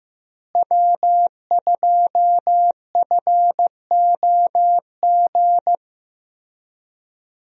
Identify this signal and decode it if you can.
Morse code, heard as a keyed tone, about 15 wpm, 700 Hz. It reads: W2FOG
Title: USA Call Signs - 5 Characters - Repeated 3x - 15wpm